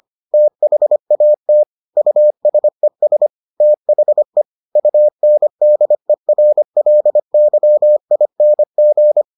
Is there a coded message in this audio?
Yes: Morse code from 0.3 to 9.2 s, heard as a keyed tone, about 25 wpm, 600 Hz. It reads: THAT USES THE UNDERLYING